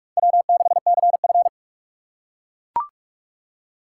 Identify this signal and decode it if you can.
Morse code, heard as a keyed tone, about 43 wpm, 700 Hz. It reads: W6CF
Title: USA Call Signs - 4 Characters - 3x Character Spacing - 45wpm